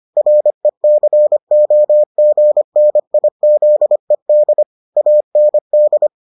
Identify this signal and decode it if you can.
Morse code, heard as a keyed tone, about 25 wpm, 600 Hz. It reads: RECOGNIZED AND